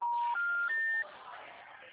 Special information tone, with noise.